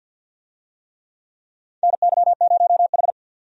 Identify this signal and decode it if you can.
Morse code, heard as a keyed tone, about 47 wpm, 700 Hz. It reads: NY0H
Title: USA Call Signs - 4 Characters - 50wpm